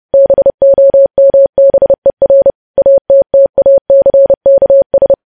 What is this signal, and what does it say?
Morse code, heard as a keyed tone, about 30 wpm, 550 Hz. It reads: BOMBER ATTACKS
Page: news headlines in morse code at 30 wpm